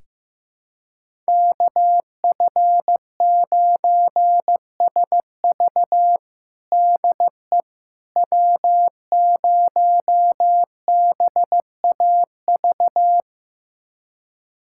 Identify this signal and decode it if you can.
Morse code, heard as a keyed tone, about 15 wpm, 700 Hz. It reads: KF9SV DE W0BAV
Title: USA Callsign DE Callsign - 5 Characters - 15wpm